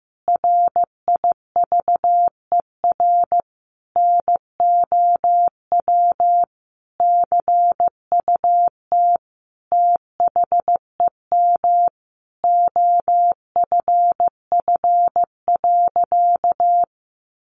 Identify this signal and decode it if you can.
Morse code, heard as a keyed tone, about 15 wpm, 700 Hz. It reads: RIVER NOW CUT THEM OFF.
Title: Sentences from Top 1000 Words 15wpm